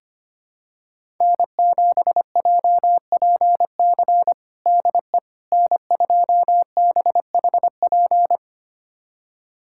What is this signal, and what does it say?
Morse code, heard as a keyed tone, about 25 wpm, 700 Hz. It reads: N7JPC DE N2BHP
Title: USA Callsign DE Callsign - 5 Characters - 25wpm